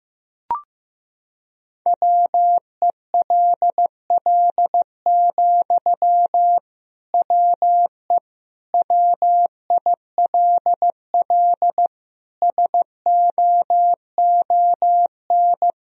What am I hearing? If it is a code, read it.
Morse code, heard as a keyed tone, about 15 wpm, 700 Hz. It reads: WELL, WE WILL SOON
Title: Sentences from Top 1000 Words - Review Part 4 - 15wpm